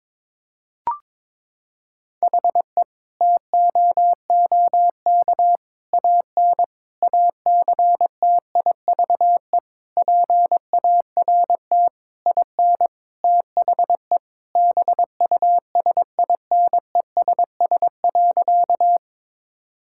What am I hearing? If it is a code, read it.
Morse code, heard as a keyed tone, about 21 wpm, 700 Hz. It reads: HE TOOK AN ACTIVE PART IN THE BUSINESS.
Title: Sentences from Top 1000 Words - Review Part 6 - 22wpm